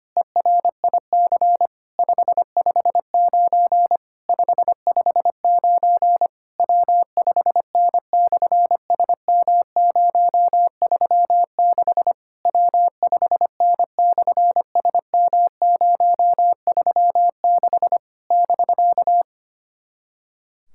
Morse code, heard as a keyed tone, about 25 wpm, 700 Hz. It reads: ERIC 559 559 W5N/SM036 W5N/SM036 <BK>
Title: SOTA QSO Practice 25wpm